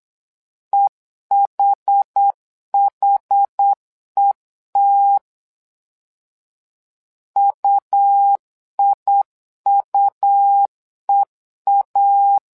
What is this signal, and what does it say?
Morse code, heard as a keyed tone, about 8 wpm, 800 Hz. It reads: EHHET UIUEA
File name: LEZIONI / 2 UHDN / 2 UH 8wpm 2129